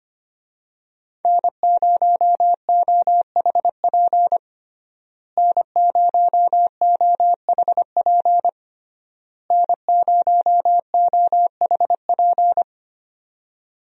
Morse code, heard as a keyed tone, about 25 wpm, 700 Hz. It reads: N0OHP N0OHP N0OHP
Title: USA Call Signs - 5 Characters - Repeated 3x - 25wpm